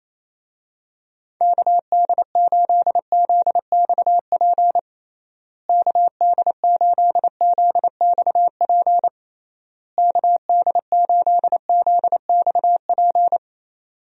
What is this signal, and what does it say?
Morse code, heard as a keyed tone, about 28 wpm, 700 Hz. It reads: KD8ZXP KD8ZXP KD8ZXP
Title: USA Call Signs - 6 Characters - Repeated 3x - 28wpm